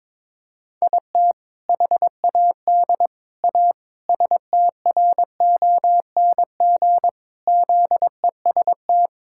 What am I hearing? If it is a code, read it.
Morse code, heard as a keyed tone, about 22 wpm, 700 Hz. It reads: IT HAD A STRONG ZEST
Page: morse-code-ninja-practice-22wpm